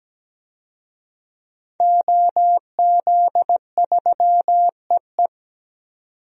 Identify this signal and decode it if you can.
Morse code, heard as a keyed tone, about 17 wpm, 700 Hz. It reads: OZ3EE
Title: DXCC-100 Call Signs - 5 Characters - 17wpm